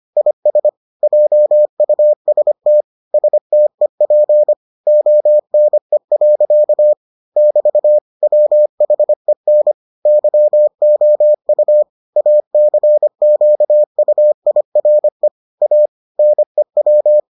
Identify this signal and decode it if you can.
Morse code, heard as a keyed tone, about 25 wpm, 600 Hz. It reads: IS JUST STEP ONE. <BT> WHEN YOU ACQUIRE A NEW